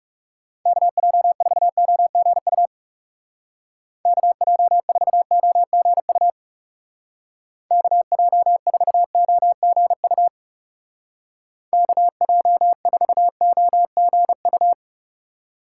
Morse code, heard as a keyed tone, about 39 wpm, 700 Hz. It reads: KJ4OGU KJ4OGU KJ4OGU KJ4OGU